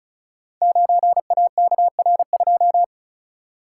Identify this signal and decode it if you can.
Morse code, heard as a keyed tone, about 36 wpm, 690 Hz. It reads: 9AKR2
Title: Code Groups - 5 Letter:Numbers - 35wpm